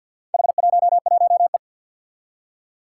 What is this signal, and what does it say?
Morse code, heard as a keyed tone, about 47 wpm, 700 Hz. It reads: H11E
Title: Four Challenging Letter-Number Combinations 50wpm